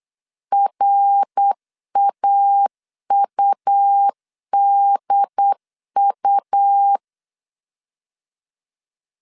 Morse code, heard as a keyed tone, about 8 wpm, 800 Hz. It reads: RAUDU